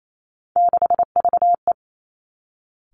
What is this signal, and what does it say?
Morse code, heard as a keyed tone, about 28 wpm, 700 Hz. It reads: 6VE
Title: Three Challenging Letter-Number Combinations 28wpm